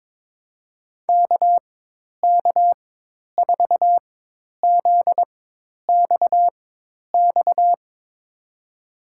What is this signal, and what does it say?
Morse code, heard as a keyed tone, about 22 wpm, spaced as for 10 wpm, 700 Hz. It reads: KK4ZXX
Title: USA Call Signs - 4x Character Spacing - 22wpm